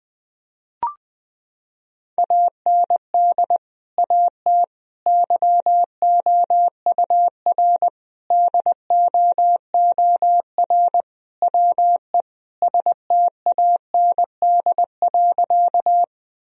Morse code, heard as a keyed tone, about 21 wpm, 700 Hz. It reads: AND AT YOUR DOOR WE STAND.
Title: Sentences from Top 1000 Words - Review Part 1 - 20wpm